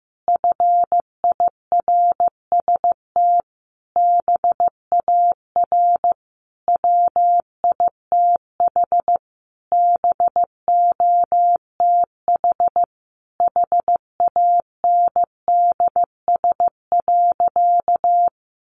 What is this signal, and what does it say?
Morse code, heard as a keyed tone, about 15 wpm, 700 Hz. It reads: FIRST BAR WITH BOTH HANDS.
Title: Easy Phrases 15wpm